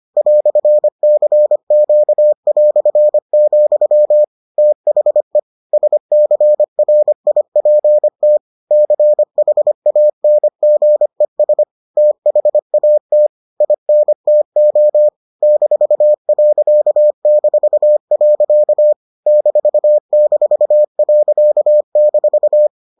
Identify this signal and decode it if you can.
Morse code, heard as a keyed tone, about 25 wpm, 600 Hz. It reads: "CQ", THE SCRIPT CHANGES THAT INTO -.-. --.-